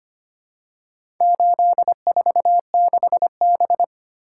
Morse code, heard as a keyed tone, about 25 wpm, 700 Hz. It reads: 846B
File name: practice / 25wpm / Four Letter-Number Combinations - 25wpm